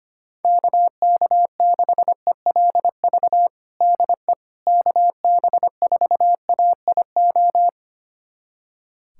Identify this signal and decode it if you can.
Morse code, heard as a keyed tone, about 25 wpm, 700 Hz. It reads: KK6ELV DE KB4AIO
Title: USA Callsign DE Callsign - 6 Characters - 25wpm